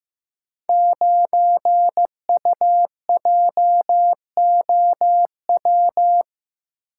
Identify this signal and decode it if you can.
Morse code, heard as a keyed tone, about 15 wpm, 700 Hz. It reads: 9UJOW
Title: Code Groups - 5 Letter:Numbers - 15wpm